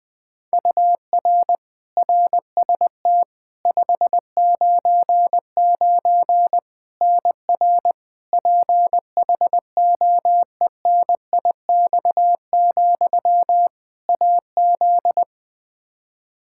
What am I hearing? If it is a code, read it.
Morse code, heard as a keyed tone, about 20 wpm, 700 Hz. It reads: UR RST 599 NR PHOENIX, AZ
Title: Practice QSO - Exchange 1 - Beginning - 20wpm